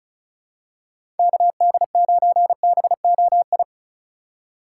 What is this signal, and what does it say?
Morse code, heard as a keyed tone, about 36 wpm, 700 Hz. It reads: KD9BOI
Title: USA Call Signs - 4x Character Spacing - 35wpm